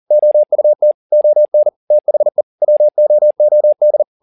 Morse code, heard as a keyed tone, about 40 wpm, 600 Hz. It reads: OUT ON THE WOOD
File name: the old man and the sea 2 40wpm